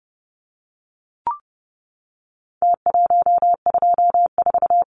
Morse code, heard as a keyed tone, about 30 wpm, 700 Hz. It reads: T124
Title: Numbers - Contest Style - 4-Digit 30wpm